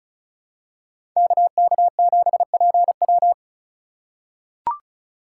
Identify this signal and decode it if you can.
Morse code, heard as a keyed tone, about 36 wpm, 700 Hz. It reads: KK7PW
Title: USA Call Signs - 5 Characters - 2x Character Spacing - 35wpm